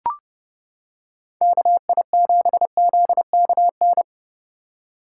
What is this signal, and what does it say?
Morse code, heard as a keyed tone, about 29 wpm, 700 Hz. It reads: KI7ZKN